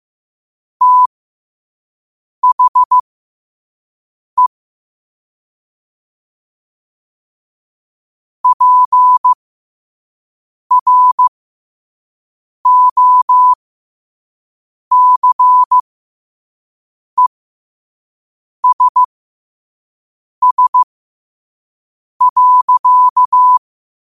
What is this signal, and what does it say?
Morse code, heard as a keyed tone, about 15 wpm, spaced as for 5 wpm, 1000 Hz. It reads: THE PROCESS.
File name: qotdfiles / 5wpm